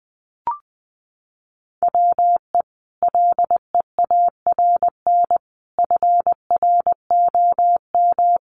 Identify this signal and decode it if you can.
Morse code, heard as a keyed tone, about 20 wpm, 700 Hz. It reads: WE LEARN FROM
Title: Sentences from Top 900 Words 20wpm